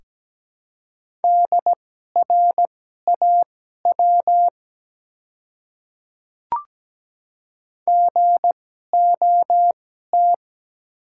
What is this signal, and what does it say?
Morse code, heard as a keyed tone, about 17 wpm, spaced as for 12 wpm, 700 Hz. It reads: DRAW GOT
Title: 2x Character Spacing - Top 300 Words - 17wpm